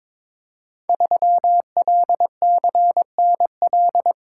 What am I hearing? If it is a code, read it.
Morse code, heard as a keyed tone, about 22 wpm, 700 Hz. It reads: 3LCNL
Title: Code Groups - 5 Letter:Numbers - 22wpm